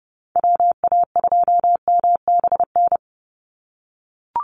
Morse code, heard as a keyed tone, about 30 wpm, 700 Hz. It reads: WA2MBN
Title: USA Call Signs 30wpm